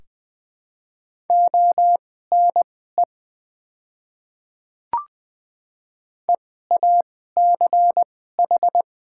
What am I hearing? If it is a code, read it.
Morse code, heard as a keyed tone, about 20 wpm, spaced as for 14 wpm, 700 Hz. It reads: ONE EACH
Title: 2x Character Spacing - Top 100 Words - 20wpm